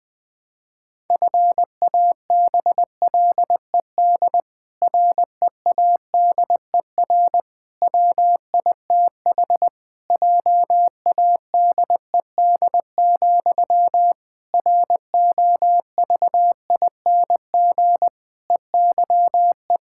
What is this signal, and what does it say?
Morse code, heard as a keyed tone, about 20 wpm, 700 Hz. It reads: FABLED READER WITH JADED, ROVING EYE
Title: Pangram Challenge v2 - 20wpm